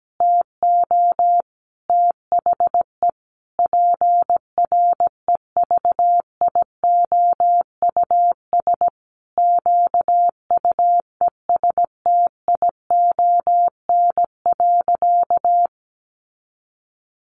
Morse code, heard as a keyed tone, about 17 wpm, 700 Hz. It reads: TO THE PREVIOUS QUESTION.
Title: Sentences from Top 1500 Words 17wpm